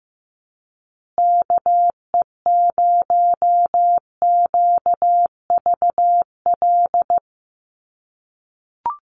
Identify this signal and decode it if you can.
Morse code, heard as a keyed tone, about 15 wpm, 700 Hz. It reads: KE0QVL
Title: USA Call Signs Repeated 2x 15wpm